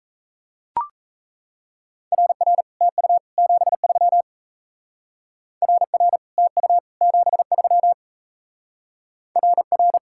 Morse code, heard as a keyed tone, about 33 wpm, 700 Hz. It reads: RR TU 73 RR TU 73 RR
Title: Speed-Racing Common QSO Phrases - 28wpm